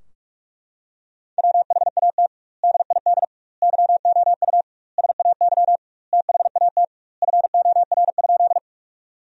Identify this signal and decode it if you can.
Morse code, heard as a keyed tone, about 41 wpm, 700 Hz. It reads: WHAT DID YOU SAY THAT FOR?
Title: 2x Word-Spacing - Sentences from Top 200 Words - 45wpm